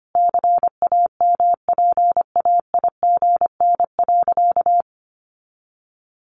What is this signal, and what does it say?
Morse code, heard as a keyed tone, about 25 wpm, 700 Hz. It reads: CAMPAIGN.
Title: Sentences from Top 1200 Words 25wpm